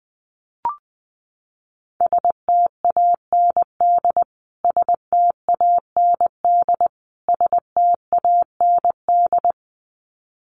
Morse code, heard as a keyed tone, about 20 wpm, 700 Hz. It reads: STAND STAND STAND
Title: Top 300 Words Repeated 20wpm